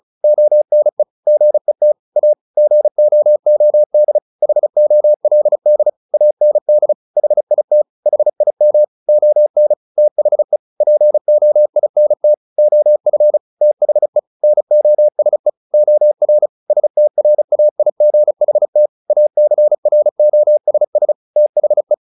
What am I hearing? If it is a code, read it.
Morse code, heard as a keyed tone, about 35 wpm, 600 Hz. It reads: ONE GET A GOOD HOLD AND HIT HIM ON THE POINT OF THE NOSE OR STRAIGHT ACROSS THE
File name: the old man and the sea 2 35wpm